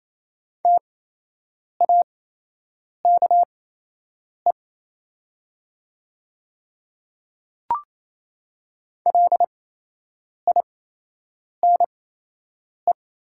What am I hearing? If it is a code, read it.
Morse code, heard as a keyed tone, about 28 wpm, spaced as for 8 wpm, 700 Hz. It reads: TAKE LINE
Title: 8x Character Spacing - Top 500 Words - 28wpm